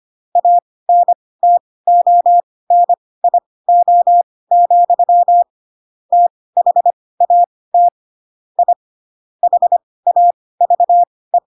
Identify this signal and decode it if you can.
Morse code, heard as a keyed tone, about 25 wpm, spaced as for 18 wpm, 700 Hz. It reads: ANTONIO, THAT I HAVE
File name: Pinocchio / Pinocchio ch2 12 25 ik1hge